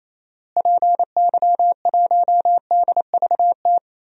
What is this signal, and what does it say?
Morse code, heard as a keyed tone, about 28 wpm, 700 Hz. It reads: PY1DVT
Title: DXCC-100 Call Signs - 6 Characters - Repeated 3x - 28wpm